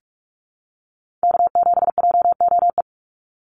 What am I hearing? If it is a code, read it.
Morse code, heard as a keyed tone, about 44 wpm, 700 Hz. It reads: K7JOE
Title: Call Signs Repeated 2x - 45wpm